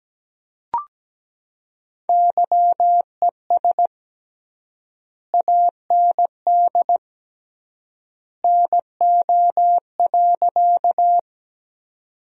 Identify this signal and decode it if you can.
Morse code, heard as a keyed tone, about 17 wpm, 700 Hz. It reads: YES AND NO.
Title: 3x Word-Spacing - Sentences from Top 500 Words - 17wpm